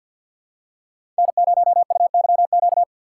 Morse code, heard as a keyed tone, about 51 wpm, 700 Hz. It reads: N0UYQ
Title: USA Call Signs - 5 Characters - 6x Character Spacing - 50wpm